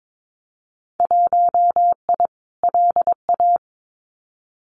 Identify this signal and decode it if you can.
Morse code, heard as a keyed tone, about 22 wpm, 700 Hz. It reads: 1I LA
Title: Winter Field Day Exchanges Encore 22wpm